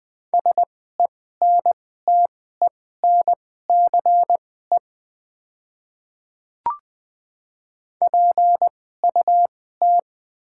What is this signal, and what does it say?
Morse code, heard as a keyed tone, about 20 wpm, spaced as for 14 wpm, 700 Hz. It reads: SENTENCE PUT
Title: 2x Character Spacing - Top 300 Words - 20wpm